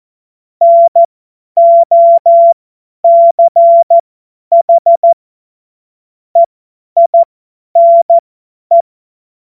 Morse code, heard as a keyed tone, about 14 wpm, spaced as for 10 wpm, 680 Hz. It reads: NOCH EINE
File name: Morsekurs / Witze / Witz 06. Dezember 2022 50 F70